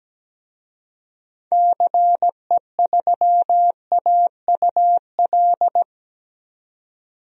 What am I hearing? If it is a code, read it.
Morse code, heard as a keyed tone, about 17 wpm, 700 Hz. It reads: CE3AUL
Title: DXCC-100 Call Signs - 6 Characters - Repeated 2x - 17wpm